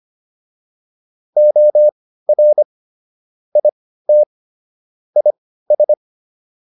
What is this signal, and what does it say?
Morse code, heard as a keyed tone, about 25 wpm, spaced as for 15 wpm, 600 Hz. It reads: OR IT IS